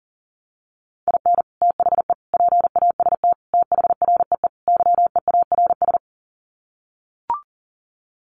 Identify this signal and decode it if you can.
Morse code, heard as a keyed tone, about 40 wpm, 700 Hz. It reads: IN THE PAST THREE YEARS
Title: Sets of 5 Words Encore2 40wpm